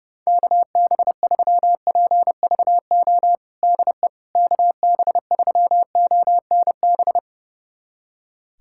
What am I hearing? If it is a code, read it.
Morse code, heard as a keyed tone, about 32 wpm, 700 Hz. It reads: KB3PVO DE KB3ONB
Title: USA Callsign DE Callsign - 6 Characters - 30wpm